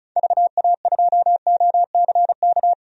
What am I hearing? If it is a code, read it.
Morse code, heard as a keyed tone, about 36 wpm, 690 Hz. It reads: VA2OCK
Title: DXCC-100 Call Signs - 6 Characters - 35wpm